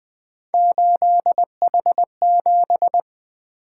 Morse code, heard as a keyed tone, about 20 wpm, 700 Hz. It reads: 8H7
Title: Three Letter-Number Combinations - 20wpm